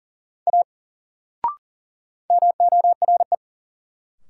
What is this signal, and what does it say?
Morse code, heard as a keyed tone, about 37 wpm, 700 Hz. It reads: A MORE
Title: Sentences from Top 100 Words - New Word Drill - 40wpm